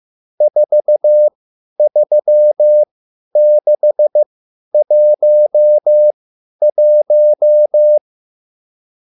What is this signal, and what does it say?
Morse code, heard as a keyed tone, about 15 wpm, spaced as for 10 wpm, 600 Hz. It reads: 43611
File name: QSL / CW lesson